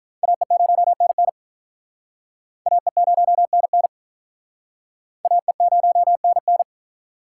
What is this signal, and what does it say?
Morse code, heard as a keyed tone, about 45 wpm, 700 Hz. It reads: AE0NN AE0NN AE0NN